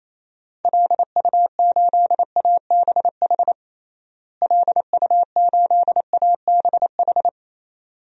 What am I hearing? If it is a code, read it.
Morse code, heard as a keyed tone, about 28 wpm, 700 Hz. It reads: LU8ABH LU8ABH
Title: DXCC-100 Call Signs - 6 Characters - Repeated 2x - 28wpm